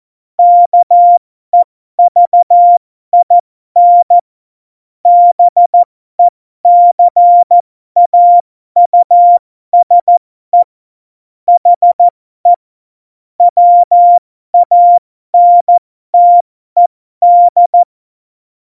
Morse code, heard as a keyed tone, about 14 wpm, spaced as for 12 wpm, 700 Hz. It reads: KEVIN BECAUSE HE WANTED
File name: Morsekurs / Witze